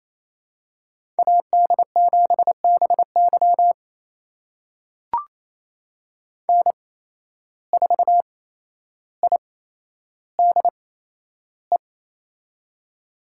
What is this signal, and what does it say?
Morse code, heard as a keyed tone, about 28 wpm, spaced as for 12 wpm, 700 Hz. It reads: AD7BY N4IDE
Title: USA Call Signs - 5 Characters - 8x Character Spacing - 28wpm